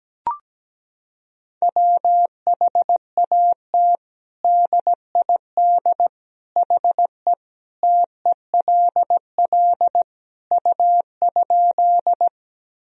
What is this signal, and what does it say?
Morse code, heard as a keyed tone, about 17 wpm, 700 Hz. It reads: WHAT DID HE TELL U?
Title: Sentences from Top 200 Words and Abbreviations 17wpm